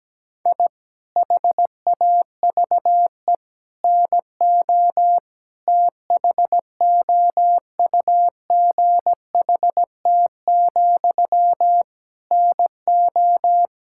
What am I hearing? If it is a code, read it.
Morse code, heard as a keyed tone, about 17 wpm, 700 Hz. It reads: I HAVE NO THOUGHT, NO
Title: Sentences from Top 1000 Words - Review Part 6 - 17wpm